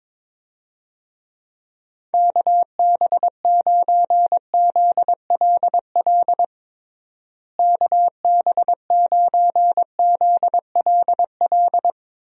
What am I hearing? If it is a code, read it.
Morse code, heard as a keyed tone, about 22 wpm, 700 Hz. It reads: KB9ZLL KB9ZLL
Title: USA Call Signs - 6 Characters - Repeated 3x - 22wpm